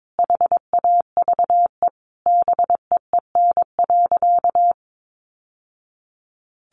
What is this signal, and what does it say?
Morse code, heard as a keyed tone, about 22 wpm, 700 Hz. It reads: HAVE BEEN.
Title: Sentences from Top 1000 Words 22wpm